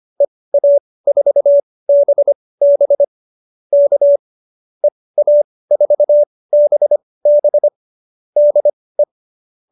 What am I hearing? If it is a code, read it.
Morse code, heard as a keyed tone, about 25 wpm, spaced as for 18 wpm, 570 Hz. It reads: EA4BB K EA4BB DE